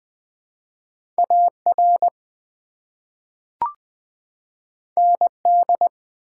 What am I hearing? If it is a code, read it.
Morse code, heard as a keyed tone, about 21 wpm, 700 Hz. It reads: AR ND
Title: ARRL Sections v2 - 20wpm